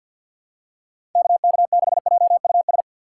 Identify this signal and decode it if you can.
Morse code, heard as a keyed tone, about 47 wpm, 700 Hz. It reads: KK6JUS
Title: USA Call Signs - 6 Characters - 6x Character Spacing - 50wpm